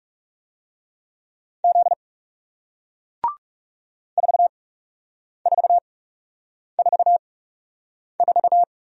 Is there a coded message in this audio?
Yes: Morse code from 1.6 to 8.6 s, heard as a keyed tone, about 39 wpm, spaced as for 8 wpm, 700 Hz. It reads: Z 4444